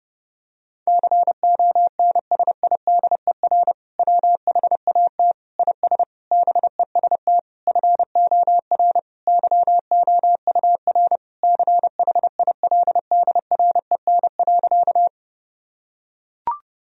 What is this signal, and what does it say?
Morse code, heard as a keyed tone, about 32 wpm, 700 Hz. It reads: CONSIDER WHAT IS BEST FOR YOUR CHILDREN.
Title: Sentences from Top 1000 Words - Review Part 4 - 30wpm